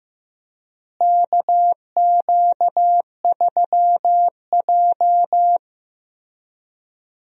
Morse code, heard as a keyed tone, about 15 wpm, 700 Hz. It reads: KQ3J
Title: USA Call Signs - 4 Characters - Repeated 3x - 15wpm